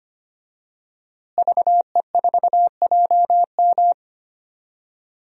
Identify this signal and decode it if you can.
Morse code, heard as a keyed tone, about 25 wpm, 700 Hz. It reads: VE4JM
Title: DXCC-100 Call Signs - 5 Characters - Repeated 3x - 25wpm